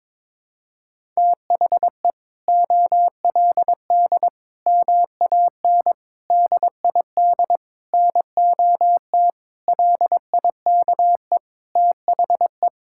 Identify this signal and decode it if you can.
Morse code, heard as a keyed tone, about 22 wpm, 700 Hz. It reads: THE OLD MAN DID NOT LIKE THE
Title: Sentences from Top 1000 Words - Review Part 6 - 22wpm